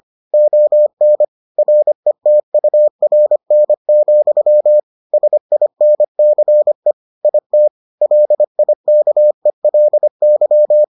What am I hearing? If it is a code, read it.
Morse code, heard as a keyed tone, about 25 wpm, 600 Hz. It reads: ON RETURN, SINCE IT LIKELY